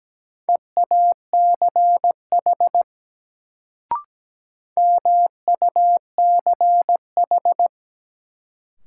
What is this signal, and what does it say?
Morse code, heard as a keyed tone, about 17 wpm, 700 Hz. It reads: EACH MUCH
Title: Sets of 2 Words - New Word Drill - 17wpm